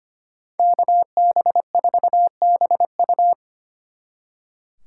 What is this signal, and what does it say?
Morse code, heard as a keyed tone, about 31 wpm, 700 Hz. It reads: KB4BU
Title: Speed Racing - USA 2x2 Call Signs 25wpm